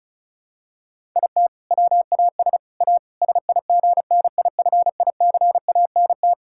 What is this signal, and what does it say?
Morse code, heard as a keyed tone, about 34 wpm, 700 Hz. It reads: IT WAS A SIGNIFICANT
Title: Sentences from Top 1000 Words - Review Part 6 - 35wpm